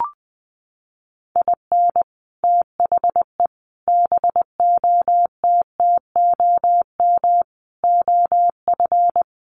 Morse code, heard as a keyed tone, about 20 wpm, 700 Hz. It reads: IN THE BOTTOM OF
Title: Sets of 5 Words Encore 20wpm